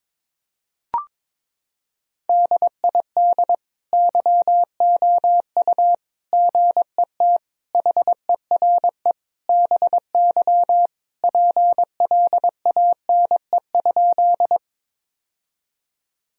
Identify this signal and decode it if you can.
Morse code, heard as a keyed tone, about 22 wpm, 700 Hz. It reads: DID YOU GET HERE BY PLANE?
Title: Sentences from Top 1000 Words - Review Part 3 - 22wpm